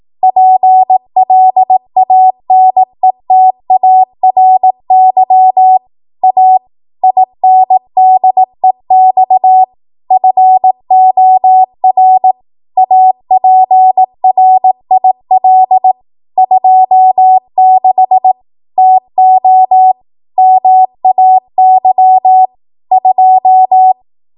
Morse code, heard as a keyed tone, about 18 wpm, 750 Hz. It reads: PLANETARY A INDEX FOR APRIL 26 TO MAY 2
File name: Morse / 18WPM